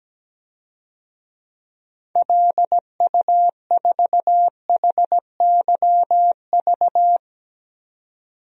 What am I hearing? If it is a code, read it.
Morse code, heard as a keyed tone, about 17 wpm, 700 Hz. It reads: LU4HYV
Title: DXCC-100 Call Signs - 6 Characters - 17wpm